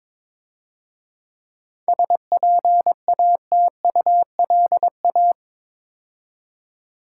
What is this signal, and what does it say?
Morse code, heard as a keyed tone, about 27 wpm, 700 Hz. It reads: SPATULA
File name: Easy Words - Speed-Racing with Hints - 22wpm